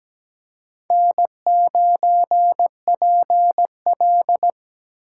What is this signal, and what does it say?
Morse code, heard as a keyed tone, about 17 wpm, 700 Hz. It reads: N9PL
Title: USA Call Signs - 4 Characters - 3x Character Spacing - 17wpm